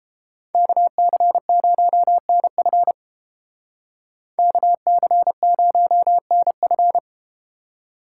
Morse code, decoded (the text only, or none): KC0NF KC0NF